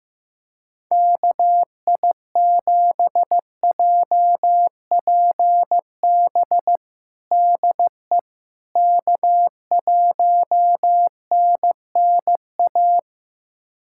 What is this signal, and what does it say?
Morse code, heard as a keyed tone, about 15 wpm, 700 Hz. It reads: KI7JPB DE K1NNA
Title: USA Callsign DE Callsign 15wpm